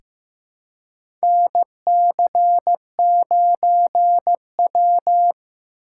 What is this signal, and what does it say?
Morse code, heard as a keyed tone, about 15 wpm, 700 Hz. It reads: NC9W
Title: K1USN SST QSO Practice - 15wpm